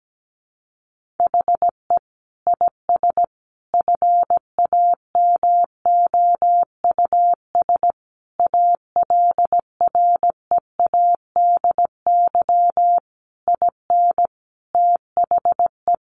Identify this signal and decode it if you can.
Morse code, heard as a keyed tone, about 17 wpm, 700 Hz. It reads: HE IS FAMOUS ALREADY IN THE
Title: Sentences from Top 1500 Words 17wpm